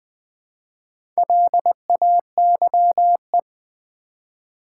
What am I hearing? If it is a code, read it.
Morse code, heard as a keyed tone, about 20 wpm, 700 Hz. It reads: LAYE
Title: Build A Word - 1 Letter At A Time - 5-Letter Words - 20wpm